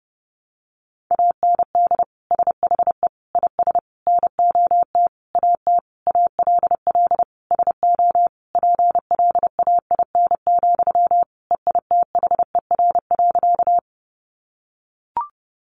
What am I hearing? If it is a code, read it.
Morse code, heard as a keyed tone, about 30 wpm, 700 Hz. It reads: AND SHE IS NOT AT ALL SO PLAIN, EITHER.
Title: Sentences from Top 600 Words 30wpm